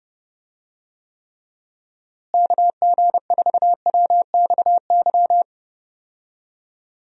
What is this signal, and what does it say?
Morse code, heard as a keyed tone, about 31 wpm, 700 Hz. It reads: KG4WXY
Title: USA Call Signs - 6 Characters - Repeated 3x - 30wpm